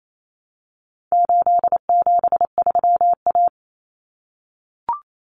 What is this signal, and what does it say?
Morse code, heard as a keyed tone, about 28 wpm, 700 Hz. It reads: 873A
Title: Common Cut Numbers - 4-Digit 28wpm